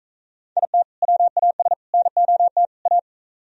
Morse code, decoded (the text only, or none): IT WAS NOT A